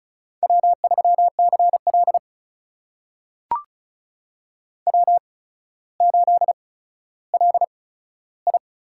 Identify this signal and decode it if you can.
Morse code, heard as a keyed tone, about 36 wpm, spaced as for 15 wpm, 700 Hz. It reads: W3CL W8LI
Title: USA Call Signs - 4 Characters - 8x Character Spacing - 35wpm